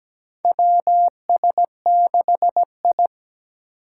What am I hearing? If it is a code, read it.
Morse code, heard as a keyed tone, about 17 wpm, 700 Hz. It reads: WS6I
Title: USA Call Signs - 4 Characters - 4x Character Spacing - 17wpm